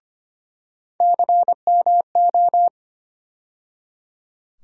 Morse code, heard as a keyed tone, about 25 wpm, 700 Hz. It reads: CMO
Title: USA Call Sign Postfixes - 3 Letters 25wpm